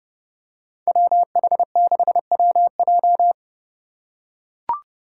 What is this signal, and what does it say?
Morse code, heard as a keyed tone, about 31 wpm, 700 Hz. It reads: WH6WJ
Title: USA Call Signs - 5 Characters - 4x Character Spacing - 30wpm